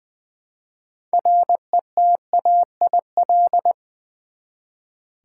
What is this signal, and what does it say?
Morse code, heard as a keyed tone, about 20 wpm, 700 Hz. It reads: RETAIL
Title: Practice QSO - Exchange 3 - More Info - 20wpm